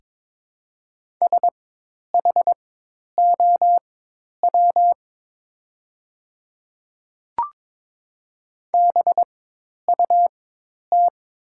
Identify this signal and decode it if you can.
Morse code, heard as a keyed tone, about 22 wpm, spaced as for 10 wpm, 690 Hz. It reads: SHOW BUT
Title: 4x Character Spacing - Top 300 Words - 22wpm